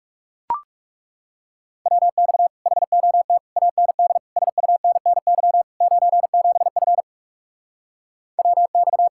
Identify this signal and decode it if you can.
Morse code, heard as a keyed tone, about 40 wpm, 700 Hz. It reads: WX HOT AND SUNNY 97F WX